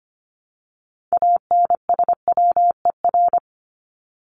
Morse code, heard as a keyed tone, about 25 wpm, 700 Hz. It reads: ANSWER
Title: Top 300 Words Repeated 25wpm